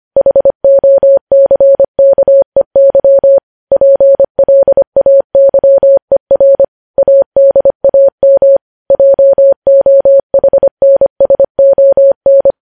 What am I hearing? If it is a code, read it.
Morse code, heard as a keyed tone, about 25 wpm, 550 Hz. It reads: HOCKEY PLAYER ADAM JOHNSON
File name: podcast / News headlines-1x-25x25